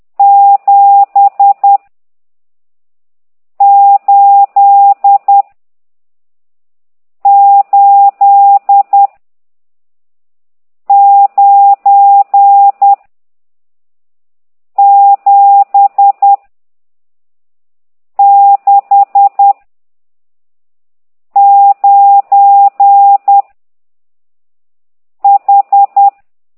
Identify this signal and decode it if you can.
Morse code, heard as a keyed tone, about 10 wpm, 800 Hz. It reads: 7 8 8 9 7 6 9 H
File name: ct2idw / exerc15